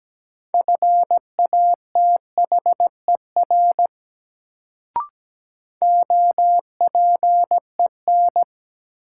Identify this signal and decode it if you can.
Morse code, heard as a keyed tone, about 17 wpm, 700 Hz. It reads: FATHER OPEN
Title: Sentences from Top 300 Words - New Word Drill - 17wpm